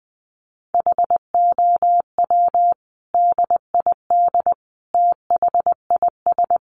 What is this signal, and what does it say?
Morse code, heard as a keyed tone, about 20 wpm, 700 Hz. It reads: HOW DID THIS
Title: Sentences from Top 400 Words 20wpm